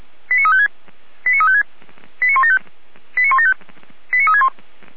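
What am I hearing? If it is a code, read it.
Five-tone selective call (ZVEI), heard as quick runs of five tones, tones of about 70 ms. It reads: 89356 89356 89175 89175 89361
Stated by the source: ZVEI five-tone selective call 89172 received with an SDR.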